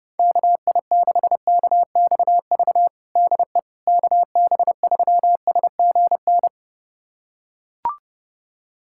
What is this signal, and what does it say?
Morse code, heard as a keyed tone, about 30 wpm, 700 Hz. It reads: KI6KXV DE KB3SGN
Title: USA Callsign DE Callsign 30wpm